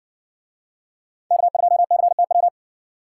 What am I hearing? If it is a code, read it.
Morse code, heard as a keyed tone, about 60 wpm, 700 Hz. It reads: K2CTW